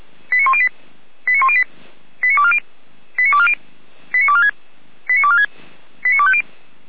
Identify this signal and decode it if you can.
Five-tone selective call (ZVEI), heard as quick runs of five tones, tones of about 70 ms. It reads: 89108 89108 89240 89240 89246 89246 89250
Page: audio.f85360.feuerwehr-orlamuende.org